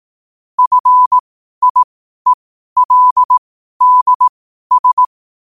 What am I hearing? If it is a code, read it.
Morse code, heard as a keyed tone, about 18 wpm, spaced as for 12 wpm, 1000 Hz. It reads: FIELDS
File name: qotdfiles / 12wpm